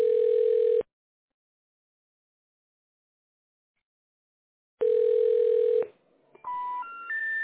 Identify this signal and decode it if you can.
Ringback tone (North America), 440 + 480 Hz.